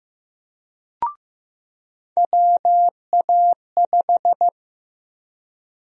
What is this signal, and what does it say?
Morse code, heard as a keyed tone, about 16 wpm, 700 Hz. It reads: WA5
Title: USA Call Sign Prefixes - Repeated 2x - 15wpm